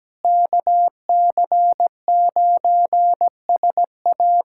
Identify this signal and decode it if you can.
Morse code, heard as a keyed tone, about 17 wpm, 700 Hz. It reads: KC9SA
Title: USA Callsign DE Callsign - 5 Characters - 17wpm